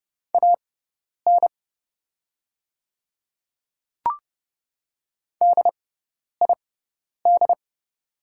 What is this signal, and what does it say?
Morse code, heard as a keyed tone, about 30 wpm, spaced as for 10 wpm, 690 Hz. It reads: AN DID
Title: 6x Character Spacing - Top 500 Words - 30wpm